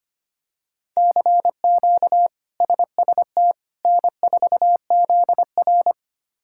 Morse code, heard as a keyed tone, about 25 wpm, 700 Hz. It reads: CQ SST N4ZR
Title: K1USN SST QSO Practice - 25wpm